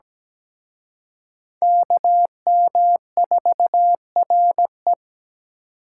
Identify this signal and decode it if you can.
Morse code, heard as a keyed tone, about 20 wpm, 700 Hz. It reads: KM4RE
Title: Speed Racing - USA 2x2 Call Signs 17wpm